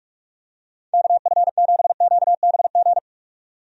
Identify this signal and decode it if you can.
Morse code, heard as a keyed tone, about 46 wpm, 700 Hz. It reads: KF7QBG
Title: USA Call Signs - 6 Characters - 45wpm